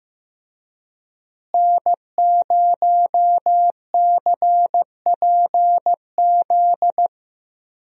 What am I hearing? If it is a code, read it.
Morse code, heard as a keyed tone, about 15 wpm, 700 Hz. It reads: N0CPZ
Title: USA Call Signs - 5 Characters - Repeated 3x - 15wpm